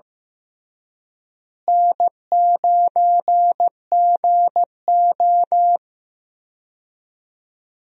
Morse code, heard as a keyed tone, about 15 wpm, 700 Hz. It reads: N9GO
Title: USA Call Signs - 4 Characters - Repeated 3x - 15wpm